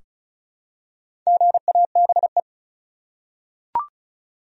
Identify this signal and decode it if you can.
Morse code, heard as a keyed tone, about 36 wpm, 700 Hz. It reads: GABE
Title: USA Common Names - 35wpm